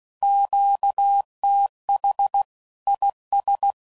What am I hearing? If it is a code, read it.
Morse code, heard as a keyed tone, about 16 wpm, 800 Hz. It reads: QTH IS
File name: CW Code Groups @15WPM / QSOs / WA7QOX QRX